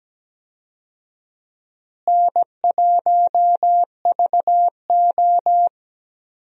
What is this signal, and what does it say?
Morse code, heard as a keyed tone, about 17 wpm, 700 Hz. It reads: N1VO
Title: USA Call Signs - 4 Characters - Repeated 2x - 17wpm